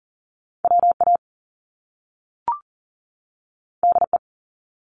Morse code, heard as a keyed tone, about 42 wpm, 700 Hz. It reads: WA DE
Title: US States 40wpm